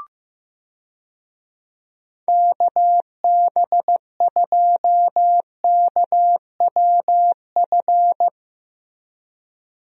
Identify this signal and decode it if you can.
Morse code, heard as a keyed tone, about 15 wpm, 700 Hz. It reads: KB2KWF
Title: USA Call Signs - 6 Characters - Repeated 2x - 15wpm